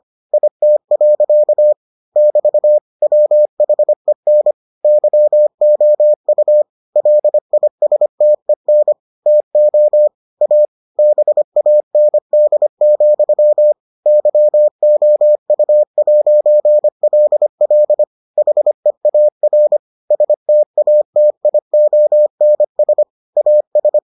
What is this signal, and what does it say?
Morse code, heard as a keyed tone, about 25 wpm, 600 Hz. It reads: IT. <BT> WHEN YOU LISTEN TO A BAND, YOU'LL HEAR STATIONS AS